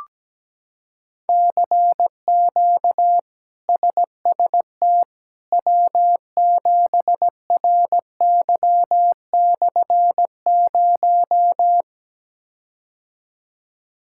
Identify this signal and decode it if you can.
Morse code, heard as a keyed tone, about 17 wpm, 700 Hz. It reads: CQ SST W7RY/0
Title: K1USN SST QSO Practice - 17wpm